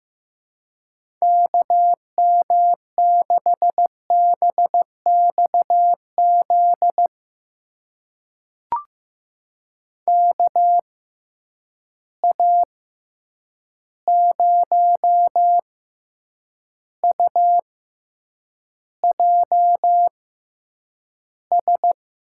Morse code, heard as a keyed tone, about 15 wpm, spaced as for 8 wpm, 700 Hz. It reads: KM6BXZ KA0UJS
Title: USA Call Signs - 6 Characters - 6x Character Spacing - 15wpm